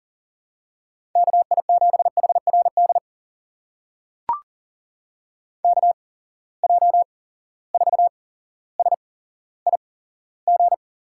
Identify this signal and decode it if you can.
Morse code, heard as a keyed tone, about 40 wpm, spaced as for 17 wpm, 700 Hz. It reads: KI7HRD KJ4SIG
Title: USA Call Signs - 6 Characters - 8x Character Spacing - 40wpm